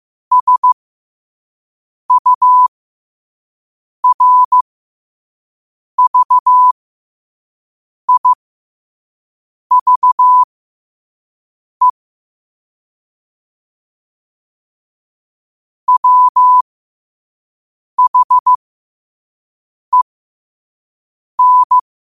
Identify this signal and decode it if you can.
Morse code, heard as a keyed tone, about 15 wpm, spaced as for 5 wpm, 1000 Hz. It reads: SURVIVE WHEN
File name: qotdfiles / 5wpm